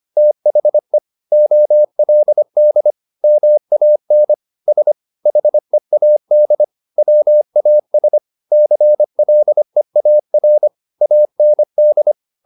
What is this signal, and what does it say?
Morse code, heard as a keyed tone, about 25 wpm, 600 Hz. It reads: THE OLD MAN S HEAD WAS CLEAR AND